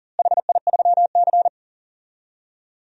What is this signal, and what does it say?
Morse code, heard as a keyed tone, about 40 wpm, 700 Hz. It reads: HI3C
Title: DXCC-100 Call Signs - 4 Characters - Repeated 2x - 40wpm